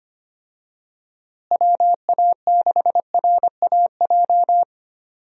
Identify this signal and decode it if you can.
Morse code, heard as a keyed tone, about 25 wpm, 700 Hz. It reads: WA6RAJ
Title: USA Call Signs - 6 Characters - Repeated 2x - 25wpm